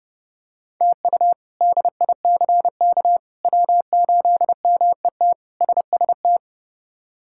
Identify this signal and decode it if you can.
Morse code, heard as a keyed tone, about 32 wpm, 700 Hz. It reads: TU DICK W8MET SST
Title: K1USN SST QSO Practice - 30wpm